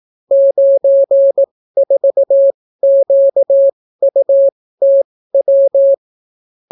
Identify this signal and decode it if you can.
Morse code, heard as a keyed tone, about 18 wpm, spaced as for 14 wpm, 550 Hz. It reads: 94QUTW